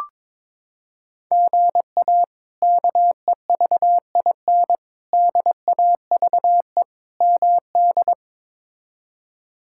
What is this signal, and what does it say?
Morse code, heard as a keyed tone, about 22 wpm, 700 Hz. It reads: GA KEVIN DAVE MD
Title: K1USN SST QSO Practice - 22wpm